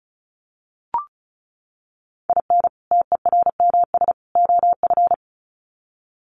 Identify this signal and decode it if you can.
Morse code, heard as a keyed tone, about 34 wpm, 700 Hz. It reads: IN TERMS OF
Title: Sets of 3 Words 35wpm